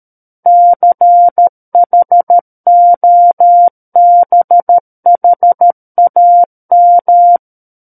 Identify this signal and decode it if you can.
Morse code, heard as a keyed tone, about 13 wpm, 700 Hz. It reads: CHOBHAM
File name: Book One / 04-Chapter 4 - 13 WPM